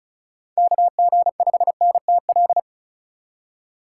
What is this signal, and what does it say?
Morse code, heard as a keyed tone, about 36 wpm, 700 Hz. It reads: KG5NTL
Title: USA Call Signs - 2x Character Spacing - 35wpm